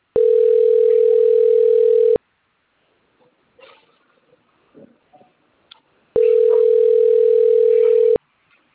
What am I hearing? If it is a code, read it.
Ringback tone (North America), 440 + 480 Hz.
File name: rg-1600-unknown- - - .1677